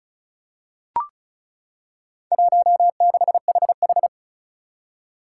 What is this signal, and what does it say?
Morse code, heard as a keyed tone, about 36 wpm, 700 Hz. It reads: 16HH
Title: Four Challenging Letter-Number Combinations 35wpm